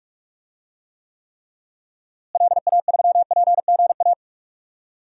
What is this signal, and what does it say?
Morse code, heard as a keyed tone, about 46 wpm, 700 Hz. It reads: LA3PGA
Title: DXCC-100 Call Signs - 6 Characters - Repeated 3x - 45wpm